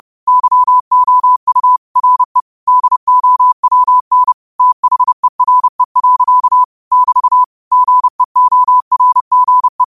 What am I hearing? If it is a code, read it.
Morse code, heard as a keyed tone, about 30 wpm, 1000 Hz. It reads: YOU RE DOWN THERE. <BT> GEORGE